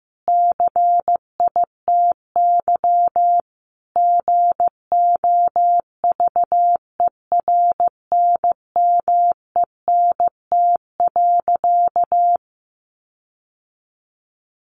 Morse code, heard as a keyed tone, about 15 wpm, 700 Hz. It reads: CITY GOVERNMENT.
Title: Sentences from Top 1400 Words 15wpm